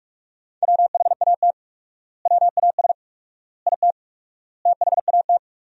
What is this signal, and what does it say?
Morse code, heard as a keyed tone, about 47 wpm, 700 Hz. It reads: WHAT WAS IT THAT
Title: 4x Word-Spacing - Sentences from Top 300 Words - 45wpm